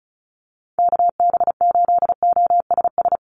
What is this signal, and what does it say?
Morse code, heard as a keyed tone, about 34 wpm, 700 Hz. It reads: KB8OSS
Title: USA Call Signs 35wpm